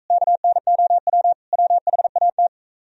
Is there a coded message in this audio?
Yes: Morse code from 0.1 to 2.5 s, heard as a keyed tone, about 45 wpm, 700 Hz. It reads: KNOW WHAT